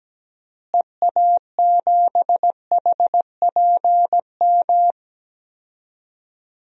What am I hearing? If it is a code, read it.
Morse code, heard as a keyed tone, about 17 wpm, 700 Hz. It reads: EA7HPM
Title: DXCC-100 Call Signs - 6 Characters - Repeated 3x - 17wpm